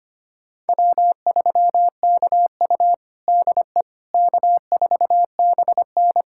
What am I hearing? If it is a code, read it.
Morse code, heard as a keyed tone, about 25 wpm, 700 Hz. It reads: W3KU DE K4BN
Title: USA Callsign DE Callsign - 4 Characters - 25wpm